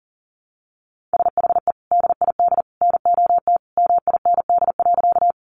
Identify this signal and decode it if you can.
Morse code, heard as a keyed tone, about 37 wpm, 700 Hz. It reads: SHE DID NOT MIND.
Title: Sentences from Top 600 Words 40wpm